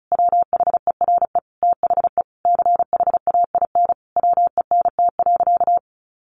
Morse code, heard as a keyed tone, about 34 wpm, 700 Hz. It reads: WHERE THE CHAIN WENT.
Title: Sentences from Top 1500 Words 35wpm